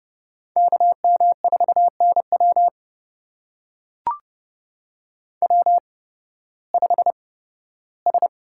Morse code, heard as a keyed tone, about 30 wpm, spaced as for 16 wpm, 700 Hz. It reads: KM4NW W5S
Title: USA Call Signs - 5 Characters - 8x Character Spacing - 30wpm